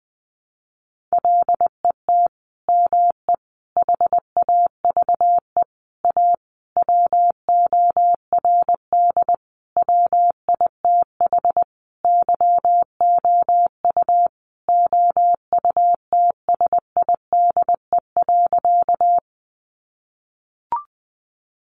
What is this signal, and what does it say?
Morse code, heard as a keyed tone, about 20 wpm, 700 Hz. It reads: LET ME HAVE A WORD WITH YOU OUTSIDE.
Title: Sentences from Top 900 Words 20wpm